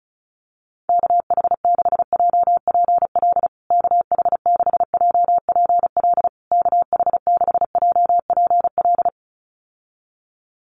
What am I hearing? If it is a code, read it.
Morse code, heard as a keyed tone, about 34 wpm, 700 Hz. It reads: KH6JPL KH6JPL KH6JPL
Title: USA Call Signs Repeated 3x 35wpm